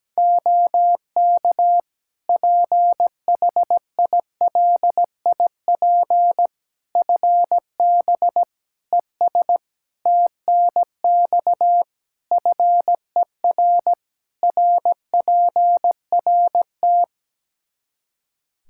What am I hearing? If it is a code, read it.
Morse code, heard as a keyed tone, about 17 wpm, 700 Hz. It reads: OK PHILIP FB ES TNX FER RPRT
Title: K1USN SST QSO Practice 17wpm